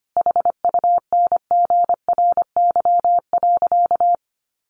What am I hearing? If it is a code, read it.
Morse code, heard as a keyed tone, about 25 wpm, 700 Hz. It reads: HUNGRY.
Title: Sentences from Top 1400 Words 25wpm